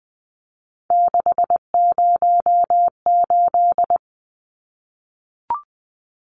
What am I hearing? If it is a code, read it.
Morse code, heard as a keyed tone, about 20 wpm, 700 Hz. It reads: 608
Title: Numbers - 3-Digit 20wpm